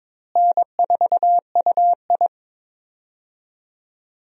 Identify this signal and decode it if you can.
Morse code, heard as a keyed tone, about 22 wpm, 700 Hz. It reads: N4UI
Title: USA Call Signs - 4 Characters - Repeated 2x - 22wpm